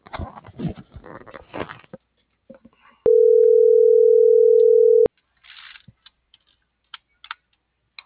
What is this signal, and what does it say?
Ringback tone (North America), 440 + 480 Hz.